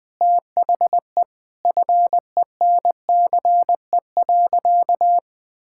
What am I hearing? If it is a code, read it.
Morse code, heard as a keyed tone, about 20 wpm, 700 Hz. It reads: THE FENCE.
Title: Sentences with 4-Word Phrases - 20wpm